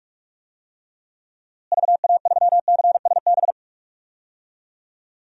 Morse code, heard as a keyed tone, about 46 wpm, 700 Hz. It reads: VA3CSB
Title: DXCC-100 Call Signs - 6 Characters - Repeated 3x - 45wpm